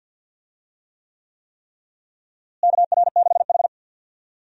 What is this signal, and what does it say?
Morse code, heard as a keyed tone, about 47 wpm, 700 Hz. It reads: KR6H
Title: USA Call Signs - 4 Characters - Repeated 3x - 50wpm